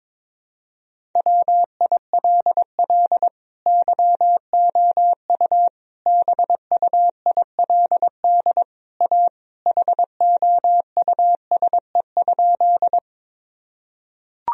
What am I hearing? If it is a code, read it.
Morse code, heard as a keyed tone, about 22 wpm, 700 Hz. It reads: WILL YOU BUILD A HOUSE?
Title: Sentences from Top 1000 Words - Review Part 1 - 22wpm